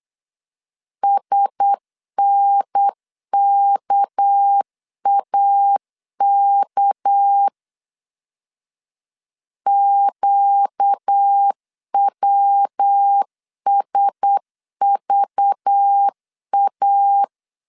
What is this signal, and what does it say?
Morse code, heard as a keyed tone, about 8 wpm, 800 Hz. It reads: SNKAK QWSVA
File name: LEZIONI / 6 PKX0 / 6 PK 8wpm 3003